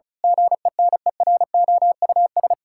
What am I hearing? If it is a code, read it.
Morse code, heard as a keyed tone, about 36 wpm, 700 Hz. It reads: GENEROUS
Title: Pangram Challenge v2 - 35wpm